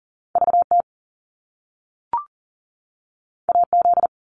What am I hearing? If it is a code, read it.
Morse code, heard as a keyed tone, about 41 wpm, 700 Hz. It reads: VT AZ
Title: US States 40wpm